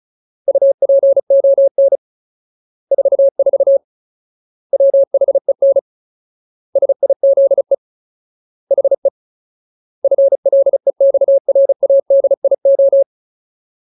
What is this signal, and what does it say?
Morse code, heard as a keyed tone, about 35 wpm, 550 Hz. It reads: UPON 44 WHEN SIZE HE FLEXRADIO
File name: Ultimate Word List / 35WPM Ultimate